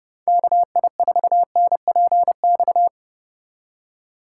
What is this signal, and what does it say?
Morse code, heard as a keyed tone, about 32 wpm, 700 Hz. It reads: KI4NPX